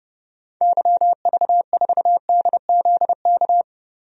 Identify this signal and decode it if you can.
Morse code, heard as a keyed tone, about 30 wpm, 700 Hz. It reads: YV4DZK
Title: DXCC-100 Call Signs - 6 Characters - 30wpm